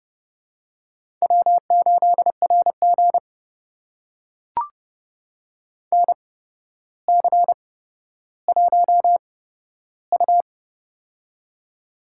Morse code, heard as a keyed tone, about 30 wpm, spaced as for 13 wpm, 700 Hz. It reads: W8RG NC1U
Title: USA Call Signs - 4 Characters - 8x Character Spacing - 30wpm